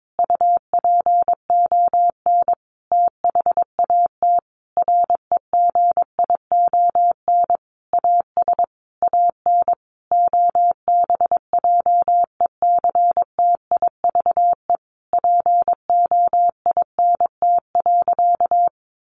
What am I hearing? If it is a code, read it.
Morse code, heard as a keyed tone, about 22 wpm, 700 Hz. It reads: UPON THAT REGION AS AN OBJECTIVE POINT.
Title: Sentences from Top 1500 Words 22wpm